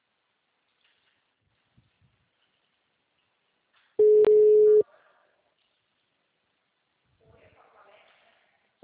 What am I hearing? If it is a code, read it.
Ringback tone (Europe, 425 Hz), with speech.